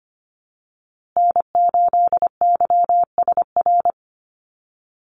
Morse code, heard as a keyed tone, about 25 wpm, 700 Hz. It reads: N8YSR
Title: USA Call Signs Repeated 3x 25wpm